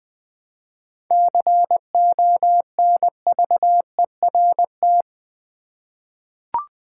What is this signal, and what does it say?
Morse code, heard as a keyed tone, about 20 wpm, 700 Hz. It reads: CONVERT
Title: 7 Letter Words - First 3 Letters Given - 20wpm